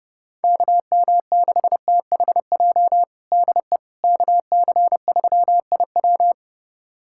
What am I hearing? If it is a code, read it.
Morse code, heard as a keyed tone, about 32 wpm, 700 Hz. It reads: KM6THJ DE KC3IW
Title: USA Callsign DE Callsign 30wpm